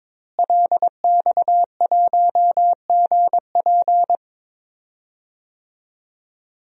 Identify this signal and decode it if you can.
Morse code, heard as a keyed tone, about 22 wpm, 700 Hz. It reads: LX1GP
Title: DXCC-100 Call Signs - 5 Characters - Repeated 2x - 22wpm